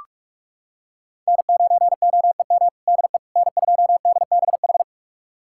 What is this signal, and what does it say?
Morse code, heard as a keyed tone, about 43 wpm, 700 Hz. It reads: N9OEM DE N2DBH
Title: USA Callsign DE Callsign - 5 Characters - 45wpm